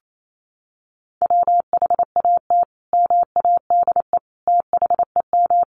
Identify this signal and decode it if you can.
Morse code, heard as a keyed tone, about 28 wpm, 700 Hz. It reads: WHAT MADE THEM
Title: Sentences from Top 1300 Words 28wpm